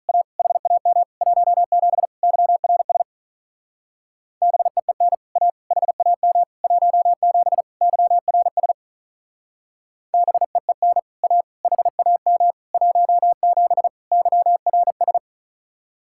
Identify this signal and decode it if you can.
Morse code, heard as a keyed tone, about 44 wpm, 700 Hz. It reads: A HAM 17 YRS BEEN A HAM 17 YRS BEEN A HAM 17 YRS
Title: Speed-Racing Common QSO Phrases - 35wpm